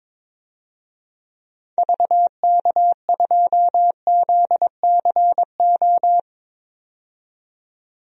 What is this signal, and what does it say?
Morse code, heard as a keyed tone, about 22 wpm, 700 Hz. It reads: VK2ZCO
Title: DXCC-100 Call Signs - 6 Characters - 22wpm